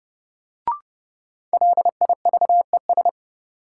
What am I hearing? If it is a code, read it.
Morse code, heard as a keyed tone, about 30 wpm, 700 Hz. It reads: LIVES
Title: Sets of 5 Words Encore - New Word Drill - 30wpm